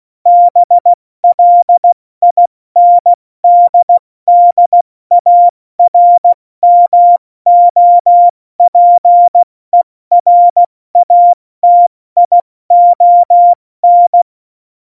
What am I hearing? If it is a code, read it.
Morse code, heard as a keyed tone, about 16 wpm, 700 Hz. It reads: BLINDDARMOPERATION
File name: Morsekurs / Witze / Witz 15. September 2020, 70 F80